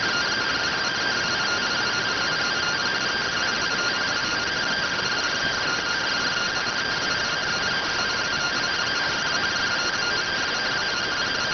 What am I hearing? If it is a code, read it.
Two-tone teleprinter signal, about 170 Hz shift.